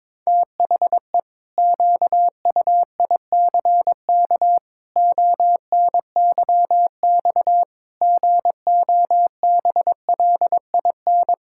Morse code, heard as a keyed tone, about 22 wpm, 700 Hz. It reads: THE QUICK ONYX GOBLIN
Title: Pangram Challenge v2 - 22wpm